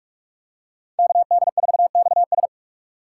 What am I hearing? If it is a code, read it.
Morse code, heard as a keyed tone, about 44 wpm, 700 Hz. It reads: KD4XS